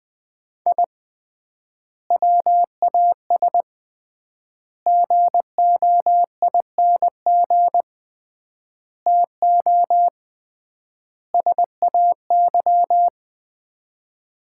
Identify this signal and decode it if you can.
Morse code, heard as a keyed tone, about 21 wpm, 700 Hz. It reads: I WAS GOING TO SAY
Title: 3x Word-Spacing - Sets of 5 Words - 20wpm